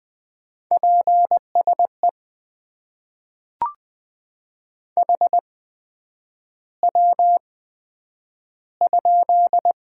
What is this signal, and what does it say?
Morse code, heard as a keyed tone, about 20 wpm, spaced as for 9 wpm, 700 Hz. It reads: PSE HW?
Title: Top 90 QSO Elements - 8x Character Spacing - 20wpm